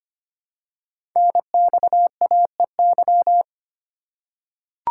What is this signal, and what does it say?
Morse code, heard as a keyed tone, about 25 wpm, 700 Hz. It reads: NXAEY
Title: Code Groups - 5 Letters - 25wpm